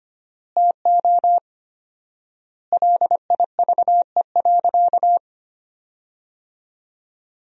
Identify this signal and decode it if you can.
Morse code, heard as a keyed tone, about 25 wpm, 700 Hz. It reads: TO LIVE.
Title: 4x Word-Spacing - Sentences from Top 300 Words - 25wpm